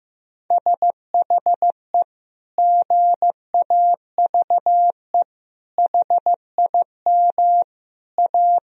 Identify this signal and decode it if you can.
Morse code, heard as a keyed tone, about 15 wpm, 700 Hz. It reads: SHE GAVE HIM A
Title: Sentences from Top 1000 Words - Review Part 2 - 15wpm